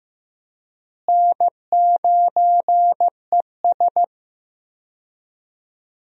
Morse code, heard as a keyed tone, about 15 wpm, 700 Hz. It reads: N9ES
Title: USA Call Signs - 4 Characters - Repeated 3x - 15wpm